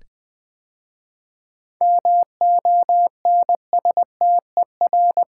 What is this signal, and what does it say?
Morse code, heard as a keyed tone, about 21 wpm, 700 Hz. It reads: MONSTER
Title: 7 Letter Words - First 3 Letters Given - 20wpm